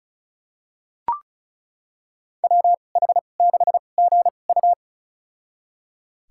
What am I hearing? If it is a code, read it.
Morse code, heard as a keyed tone, about 36 wpm, spaced as for 25 wpm, 700 Hz. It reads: WH6GU
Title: USA Call Signs - 5 Characters - 2x Character Spacing - 35wpm